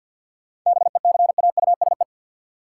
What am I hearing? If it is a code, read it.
Morse code, heard as a keyed tone, about 53 wpm, 700 Hz. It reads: BECAUSE
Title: Sets of 2 Words - New Word Drill - 50wpm